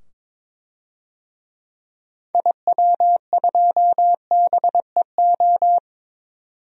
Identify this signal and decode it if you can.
Morse code, heard as a keyed tone, about 21 wpm, 700 Hz. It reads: IW2BEO
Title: DXCC-100 Call Signs - 6 Characters - 22wpm